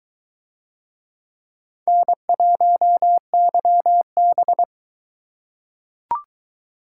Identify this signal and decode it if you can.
Morse code, heard as a keyed tone, about 22 wpm, 700 Hz. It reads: N1YB